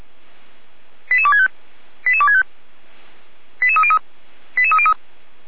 Five-tone selective call (ZVEI), heard as quick runs of five tones, tones of about 80 ms. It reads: 88265 88265 88303 88303
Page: /455/SDR/ZVEI/88255/